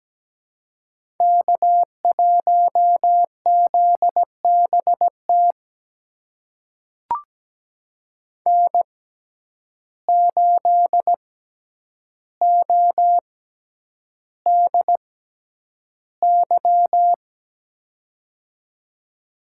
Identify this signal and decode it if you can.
Morse code, heard as a keyed tone, about 17 wpm, spaced as for 9 wpm, 700 Hz. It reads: K1ZBT N8ODY
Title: USA Call Signs - 5 Characters - 6x Character Spacing - 17wpm